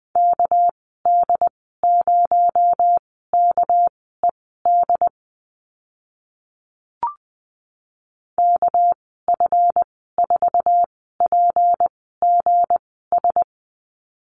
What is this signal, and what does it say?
Morse code, heard as a keyed tone, about 20 wpm, spaced as for 14 wpm, 700 Hz. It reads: KD0KED KF4PGS
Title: 238 Lesson 38g - Letter q - Call Signs 20wpm